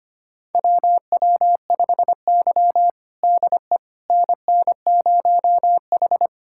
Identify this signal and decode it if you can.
Morse code, heard as a keyed tone, about 25 wpm, 700 Hz. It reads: WW5Y DE NN0H
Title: USA Callsign DE Callsign - 4 Characters - 25wpm